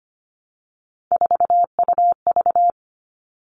Morse code, heard as a keyed tone, about 25 wpm, 700 Hz. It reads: 4UV
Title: All Cut Numbers - 3-Digit 25wpm